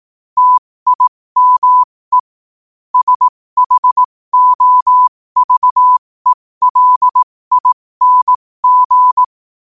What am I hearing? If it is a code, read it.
Morse code, heard as a keyed tone, about 18 wpm, spaced as for 15 wpm, 1000 Hz. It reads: TIME SHOVELING
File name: qotdfiles / 15wpm